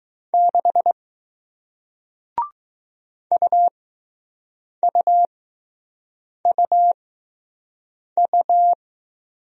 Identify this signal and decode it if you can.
Morse code, heard as a keyed tone, about 20 wpm, spaced as for 6 wpm, 700 Hz. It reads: 6 UUUU